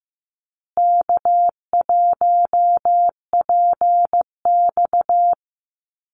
Morse code, heard as a keyed tone, about 15 wpm, 700 Hz. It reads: K1PX
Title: Easy Call Signs Repeated 3x - 15wpm